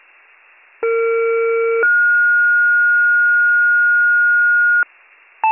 Two-tone page, heard as a one-second tone, then a three-second tone, about 460 then 1430 Hz. It is heard from 0.8 to 4.8 s.